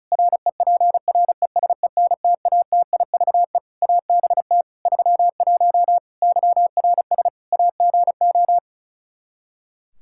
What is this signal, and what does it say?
Morse code, heard as a keyed tone, about 36 wpm, 700 Hz. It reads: REPRESENTATIVE ABT 31 YRS AGO
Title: Practice QSO - Exchange 3 - More Info - 35wpm